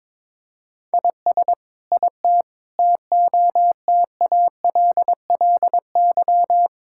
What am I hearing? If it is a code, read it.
Morse code, heard as a keyed tone, about 22 wpm, 700 Hz. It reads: IS IT TOTALLY
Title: Sentences from Top 1600 Words 22wpm